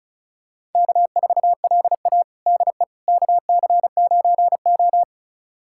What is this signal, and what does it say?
Morse code, heard as a keyed tone, about 36 wpm, 700 Hz. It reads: K4LA DE KC9O
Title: USA Callsign DE Callsign - 4 Characters - 35wpm